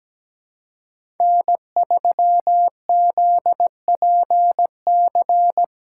Morse code, heard as a keyed tone, about 17 wpm, 700 Hz. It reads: N3ZPC
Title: USA Callsign DE Callsign - 5 Characters - 17wpm